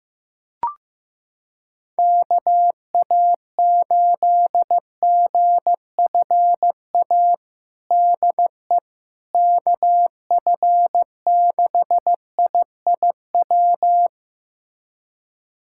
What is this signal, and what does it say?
Morse code, heard as a keyed tone, about 15 wpm, 700 Hz. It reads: KA8GFA DE KF6IIW
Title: USA Callsign DE Callsign 15wpm